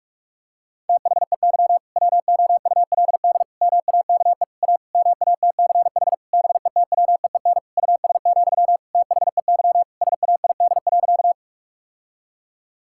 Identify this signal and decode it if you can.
Morse code, heard as a keyed tone, about 43 wpm, 700 Hz. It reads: THEY WOULD MAKE A MATCH BETWEEN US, THEY SAID.
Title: Sentences from Top 1600 Words 45wpm